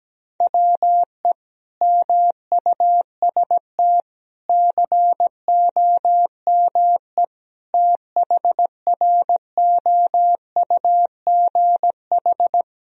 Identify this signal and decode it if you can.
Morse code, heard as a keyed tone, about 17 wpm, 700 Hz. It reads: WE MUST COME THROUGH
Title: Sentences from Top 1000 Words - Review Part 1 - 17wpm